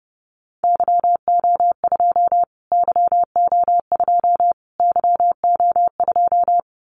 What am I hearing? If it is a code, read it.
Morse code, heard as a keyed tone, about 30 wpm, 700 Hz. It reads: YO2 YO2 YO2
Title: Callsign Prefixes for Top 25 DXCC - Repeated 3x - 30wpm